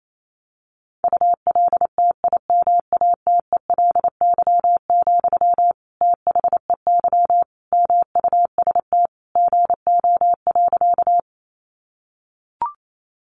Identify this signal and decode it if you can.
Morse code, heard as a keyed tone, about 28 wpm, 700 Hz. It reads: ULTIMATELY, THEY MUST GO.
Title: Sentences from Top 1200 Words 28wpm